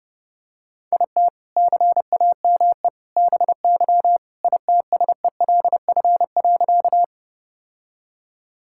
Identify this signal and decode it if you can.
Morse code, heard as a keyed tone, about 30 wpm, 700 Hz. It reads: IT CAME BY ITSELF.
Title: Sentences from Top 1000 Words - Review Part 5 - 30wpm